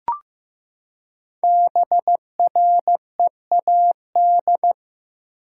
Morse code, heard as a keyed tone, about 15 wpm, 700 Hz. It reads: BREAD
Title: Five Letter Words - 15wpm